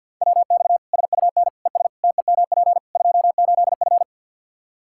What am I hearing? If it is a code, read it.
Morse code, heard as a keyed tone, about 53 wpm, 700 Hz. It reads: WX SUN ES TEMP 28F
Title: Practice QSO - Exchange 2 - Info - 50wpm